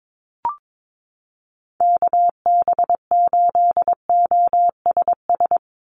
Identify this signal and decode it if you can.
Morse code, heard as a keyed tone, about 22 wpm, 700 Hz. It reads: KB8OSS
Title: USA Call Signs 22wpm